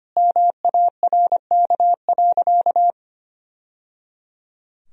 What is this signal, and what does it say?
Morse code, heard as a keyed tone, about 25 wpm, 700 Hz. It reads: MARK.
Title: Sentences from Top 1000 Words - Review Part 1 - 25wpm